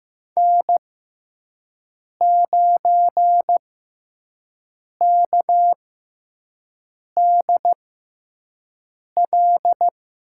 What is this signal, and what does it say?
Morse code, heard as a keyed tone, about 15 wpm, spaced as for 5 wpm, 700 Hz. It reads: N9KDL
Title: USA Call Signs - 6x Character Spacing - 15wpm